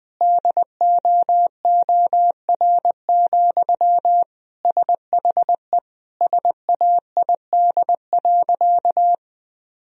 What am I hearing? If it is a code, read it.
Morse code, heard as a keyed tone, about 20 wpm, 700 Hz. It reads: DOOR, SHE SAID.
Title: Sentences from Top 1000 Words - Review Part 2 - 20wpm